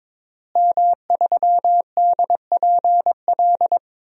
Morse code, heard as a keyed tone, about 22 wpm, 700 Hz. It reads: M3DPL
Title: DXCC-100 Call Signs - 5 Characters - Repeated 3x - 22wpm